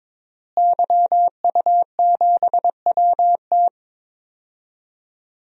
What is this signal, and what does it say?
Morse code, heard as a keyed tone, about 22 wpm, 700 Hz. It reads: YU7WT
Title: DXCC-100 Call Signs - 5 Characters - Repeated 2x - 22wpm